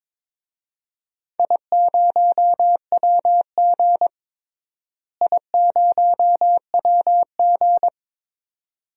Morse code, heard as a keyed tone, about 22 wpm, 700 Hz. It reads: I0WG I0WG
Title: DXCC-100 Call Signs - 4 Characters - Repeated 2x - 22wpm